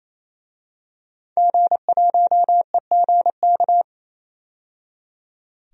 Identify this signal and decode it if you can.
Morse code, heard as a keyed tone, about 28 wpm, 700 Hz. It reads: G1EGK
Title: DXCC-100 Call Signs - 5 Characters - 28wpm